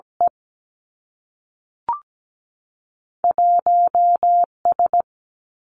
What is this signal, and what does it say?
Morse code, heard as a keyed tone, about 17 wpm, 700 Hz. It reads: E 1 S
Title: Two Challenging Letter-Number Combinations 17wpm